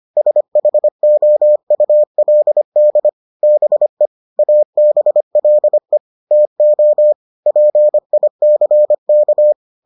Morse code, heard as a keyed tone, about 25 wpm, 600 Hz. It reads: SHOULD BE ABLE TO PICK